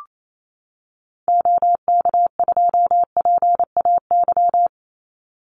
Morse code, heard as a keyed tone, about 28 wpm, 700 Hz. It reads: OK2PAY
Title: Call Signs - 28wpm